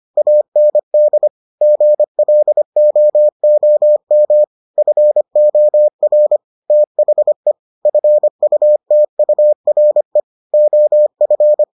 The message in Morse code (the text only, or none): AND GLOOM FOR THE FUTURE OF